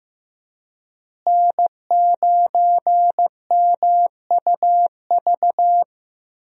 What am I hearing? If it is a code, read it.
Morse code, heard as a keyed tone, about 15 wpm, 700 Hz. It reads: N9MUV
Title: USA Call Signs - 5 Characters - Repeated 2x - 15wpm